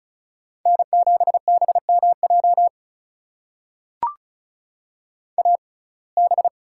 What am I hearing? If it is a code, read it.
Morse code, heard as a keyed tone, about 34 wpm, spaced as for 25 wpm, 700 Hz. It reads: N7BMJ AB
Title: USA Call Signs - 5 Characters - 6x Character Spacing - 35wpm